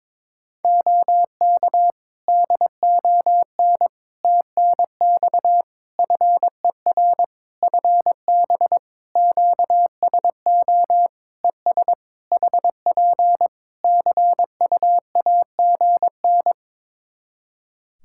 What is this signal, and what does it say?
Morse code, heard as a keyed tone, about 22 wpm, 700 Hz. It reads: OK DON TNX FER FB QSO ES HP CUAGN
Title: Practice QSO - Exchange 4 - Ending - 22wpm